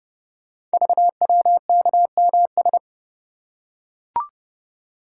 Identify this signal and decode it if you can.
Morse code, heard as a keyed tone, about 30 wpm, 700 Hz. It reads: VWKMS
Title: Code Groups - 5 Letters - 30wpm